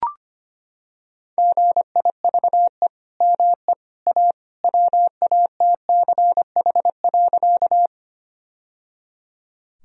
Morse code, heard as a keyed tone, about 25 wpm, 700 Hz. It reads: GIVE ME A WATCH.
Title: Sentences from Top 1000 Words - Review Part 1 - 25wpm